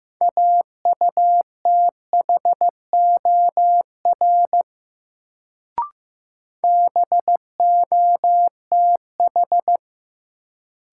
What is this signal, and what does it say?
Morse code, heard as a keyed tone, about 15 wpm, 700 Hz. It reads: AUTHOR BOTH
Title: Sets of 4 Words - New Word Drill - 15wpm